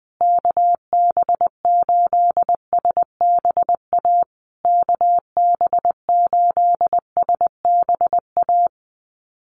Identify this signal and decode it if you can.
Morse code, heard as a keyed tone, about 19 wpm, 700 Hz. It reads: KB8SBA KB8SBA
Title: USA Call Signs Repeated 3x 20wpm